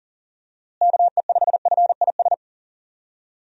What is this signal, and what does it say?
Morse code, heard as a keyed tone, about 40 wpm, 700 Hz. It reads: KE5FIS
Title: USA Call Signs - 6 Characters - Repeated 2x - 40wpm